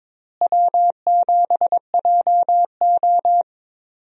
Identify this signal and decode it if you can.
Morse code, heard as a keyed tone, about 22 wpm, 700 Hz. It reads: W7JO
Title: USA Call Signs - 4 Characters - Repeated 2x - 22wpm